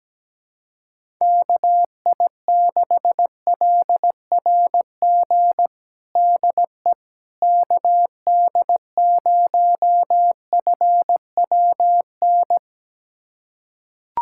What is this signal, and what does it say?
Morse code, heard as a keyed tone, about 17 wpm, 700 Hz. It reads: KI6LRG DE KD0FWN
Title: USA Callsign DE Callsign - 6 Characters - 17wpm